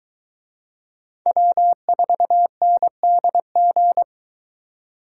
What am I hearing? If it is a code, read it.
Morse code, heard as a keyed tone, about 23 wpm, 700 Hz. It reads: W4NDG